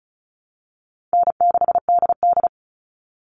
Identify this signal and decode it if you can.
Morse code, heard as a keyed tone, about 34 wpm, 700 Hz. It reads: N6DD
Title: All Cut Numbers - 4-Digit 35wpm